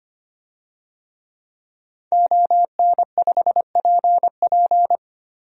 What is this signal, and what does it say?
Morse code, heard as a keyed tone, about 25 wpm, 700 Hz. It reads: ON5PP
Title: DXCC-100 Call Signs - 5 Characters - Repeated 2x - 25wpm